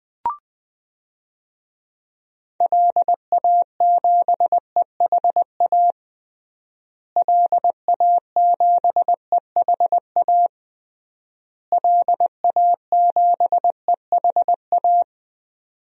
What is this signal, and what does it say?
Morse code, heard as a keyed tone, about 20 wpm, 700 Hz. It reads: LA7EHA LA7EHA LA7EHA
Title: DXCC-100 Call Signs - 6 Characters - Repeated 3x - 20wpm